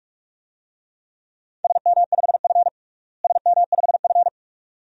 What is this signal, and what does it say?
Morse code, heard as a keyed tone, about 41 wpm, 700 Hz. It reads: SM5F SM5F
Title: DXCC-100 Call Signs - 4 Characters - Repeated 2x - 45wpm